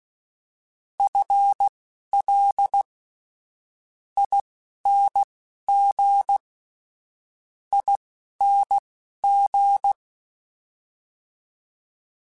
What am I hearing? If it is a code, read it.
Morse code, heard as a keyed tone, about 16 wpm, spaced as for 11 wpm, 800 Hz. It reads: FL ING ING
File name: FL / FL mp3s / FL Prefixes 3x2 sp @10